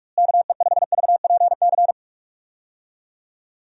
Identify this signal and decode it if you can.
Morse code, heard as a keyed tone, about 43 wpm, 700 Hz. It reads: KE5VPC
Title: USA Call Signs - 6 Characters - Repeated 3x - 45wpm